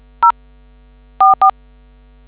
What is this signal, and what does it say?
DTMF, heard as telephone touch tones, tones of about 100 ms. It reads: 044